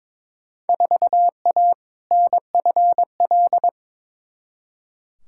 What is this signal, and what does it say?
Morse code, heard as a keyed tone, about 22 wpm, 700 Hz. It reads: 4A NFL
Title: Field Day Exchanges - 22wpm